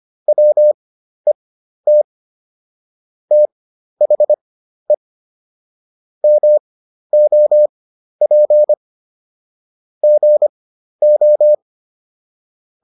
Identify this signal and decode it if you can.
Morse code, heard as a keyed tone, about 25 wpm, spaced as for 12 wpm, 600 Hz. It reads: WET THE MOP GO